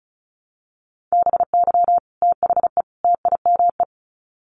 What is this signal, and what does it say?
Morse code, heard as a keyed tone, about 34 wpm, 700 Hz. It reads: BY THE TIME
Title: Sets of 3 Words 35wpm